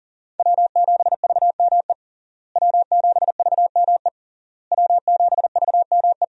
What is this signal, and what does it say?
Morse code, heard as a keyed tone, about 40 wpm, 700 Hz. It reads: W7VME W7VME W7VME
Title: USA Call Signs - 5 Characters - Repeated 3x - 40wpm